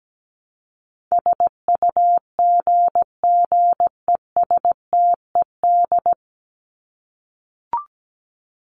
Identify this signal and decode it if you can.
Morse code, heard as a keyed tone, about 17 wpm, 700 Hz. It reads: SUGGESTED
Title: ED Suffix - Top 100 Words with Base Words 17wpm